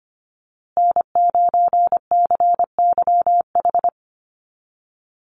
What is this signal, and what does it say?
Morse code, heard as a keyed tone, about 25 wpm, 700 Hz. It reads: N9CYH
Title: USA Call Signs Repeated 3x 25wpm